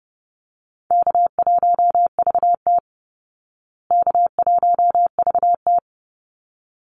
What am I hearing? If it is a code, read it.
Morse code, heard as a keyed tone, about 30 wpm, 700 Hz. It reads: K1VT K1VT
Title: Easy Call Signs Repeated 2x - 30wpm